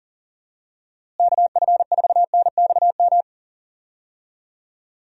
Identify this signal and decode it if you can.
Morse code, heard as a keyed tone, about 40 wpm, 700 Hz. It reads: KF4NXM
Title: USA Call Signs - 6 Characters - Repeated 2x - 40wpm